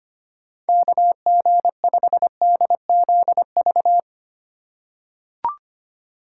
Morse code, heard as a keyed tone, about 25 wpm, 700 Hz. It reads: KG5DZV
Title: USA Call Signs - 3x Character Spacing - 25wpm